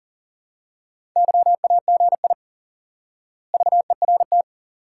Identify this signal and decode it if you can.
Morse code, heard as a keyed tone, about 40 wpm, 700 Hz. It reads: YAGI VERT
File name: IQE Course 3 - Lesson 89 - YAGI - Two Elements - 40wpm